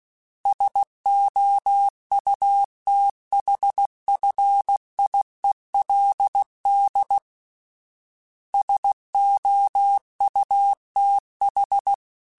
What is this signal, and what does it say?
Morse code, heard as a keyed tone, about 16 wpm, 800 Hz. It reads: SOUTHFIELD SOUTH
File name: fromSMs / FromSMs mp3s / Cities@15